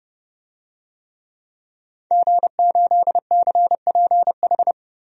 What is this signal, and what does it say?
Morse code, heard as a keyed tone, about 30 wpm, 700 Hz. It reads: G8CPH
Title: DXCC-100 Call Signs - 5 Characters - 30wpm